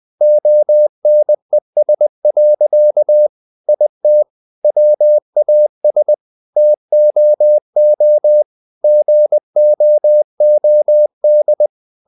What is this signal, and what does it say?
Morse code, heard as a keyed tone, about 20 wpm, 600 Hz. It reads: ONES. IT WAS TOO GOOD
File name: the old man and the sea part 2 20wpm